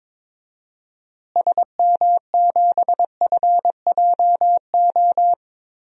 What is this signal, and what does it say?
Morse code, heard as a keyed tone, about 22 wpm, 700 Hz. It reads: SM7FJO
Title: DXCC-100 Call Signs - 6 Characters - Repeated 2x - 22wpm